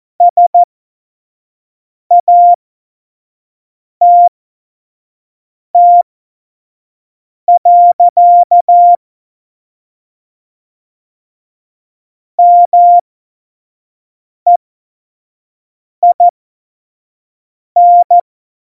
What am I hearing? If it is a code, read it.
Morse code, heard as a keyed tone, about 14 wpm, spaced as for 5 wpm, 700 Hz. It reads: SATT. MEIN
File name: Morsekurs / Witze / Witz 06. November 2018, 25 F70